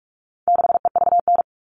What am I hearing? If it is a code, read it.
Morse code, heard as a keyed tone, about 44 wpm, 700 Hz. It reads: 6EVN
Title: All Cut Numbers - 4-Digit 45wpm